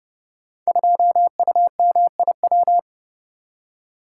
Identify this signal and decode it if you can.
Morse code, heard as a keyed tone, about 29 wpm, 700 Hz. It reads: 2UMIW
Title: Code Groups - 5 Letter:Numbers - 30wpm